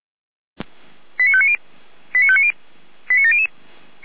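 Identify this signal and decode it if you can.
Five-tone selective call (ZVEI), heard as quick runs of five tones, tones of about 70 ms. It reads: 89590 89590 89700
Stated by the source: ZVEI five-tone call received on an SDR.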